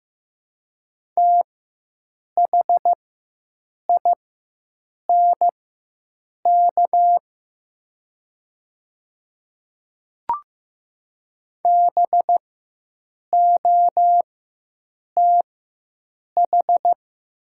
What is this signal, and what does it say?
Morse code, heard as a keyed tone, about 15 wpm, spaced as for 7 wpm, 700 Hz. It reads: THINK BOTH
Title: 4x Character Spacing - Top 300 Words - 15wpm